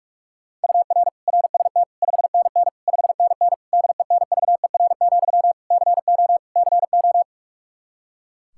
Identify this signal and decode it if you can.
Morse code, heard as a keyed tone, about 41 wpm, 700 Hz. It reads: UR RST 5NN 5NN DENVER, CO CO
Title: Practice QSO - Exchange 1 - Beginning - 45wpm